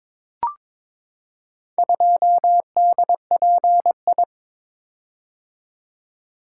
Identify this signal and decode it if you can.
Morse code, heard as a keyed tone, about 22 wpm, 700 Hz. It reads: 2DPI
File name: practice / 22wpm / Four Letter-Number Combinations - 22wpm